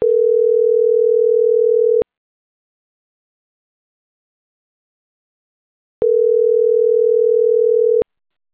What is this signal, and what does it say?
Ringback tone (North America), 440 + 480 Hz.